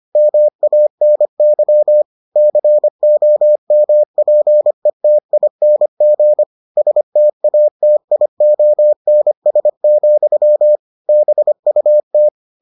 Morse code, heard as a keyed tone, about 25 wpm, 600 Hz. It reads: MANY COMPETING STATIONS, BUT